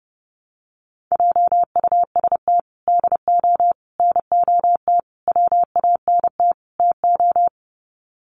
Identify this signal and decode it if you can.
Morse code, heard as a keyed tone, about 30 wpm, 700 Hz. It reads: JUST DO NOT WANT TO
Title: Sets of 5 Words Encore 30wpm